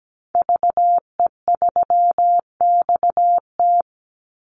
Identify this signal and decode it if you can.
Morse code, heard as a keyed tone, about 17 wpm, 700 Hz. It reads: VE3XT
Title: Call Signs Repeated 2x - 17wpm